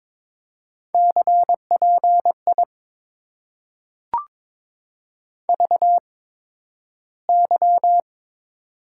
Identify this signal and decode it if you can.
Morse code, heard as a keyed tone, about 22 wpm, spaced as for 12 wpm, 700 Hz. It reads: CPI VY
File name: Top 90 QSO Elements - 8x Character Spacing - 22wpm